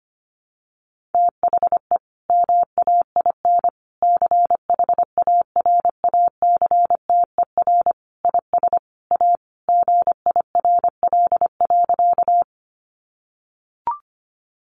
Morse code, heard as a keyed tone, about 25 wpm, 700 Hz. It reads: THE MAIN CHARACTER IS A GIRL.
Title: Additional Easy Phrases 25wpm